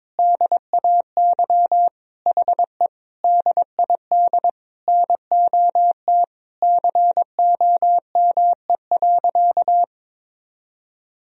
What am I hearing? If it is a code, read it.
Morse code, heard as a keyed tone, about 22 wpm, 700 Hz. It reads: DAY HE DID NOT COME.
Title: Sentences from Top 1000 Words - Review Part 4 - 22wpm